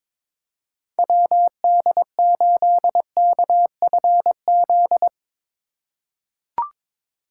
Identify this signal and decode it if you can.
Morse code, heard as a keyed tone, about 22 wpm, 700 Hz. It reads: WD8KFZ
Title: USA Call Signs - 3x Character Spacing - 22wpm